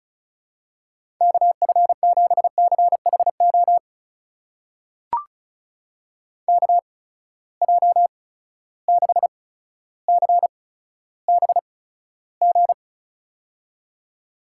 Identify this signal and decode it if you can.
Morse code, heard as a keyed tone, about 36 wpm, spaced as for 15 wpm, 700 Hz. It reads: KF7CHO KJ6CBG
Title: USA Call Signs - 6 Characters - 8x Character Spacing - 35wpm